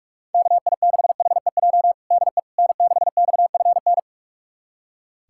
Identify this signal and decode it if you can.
Morse code, heard as a keyed tone, about 46 wpm, 700 Hz. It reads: KI6HEJ DE N6XFN
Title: USA Callsign DE Callsign 45wpm